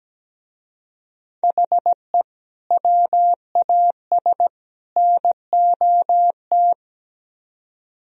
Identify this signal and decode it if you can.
Morse code, heard as a keyed tone, about 19 wpm, 700 Hz. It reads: HE WAS NOT
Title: Speed-Racing - Sets of 3 Words 17wpm